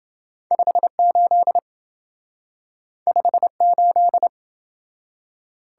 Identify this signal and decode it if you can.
Morse code, heard as a keyed tone, about 27 wpm, 700 Hz. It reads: 58 58